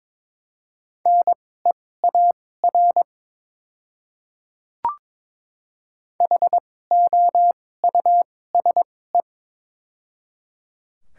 Morse code, heard as a keyed tone, about 22 wpm, spaced as for 16 wpm, 700 Hz. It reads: NEAR HOUSE
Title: 2x Character Spacing - Top 200 Words - 22wpm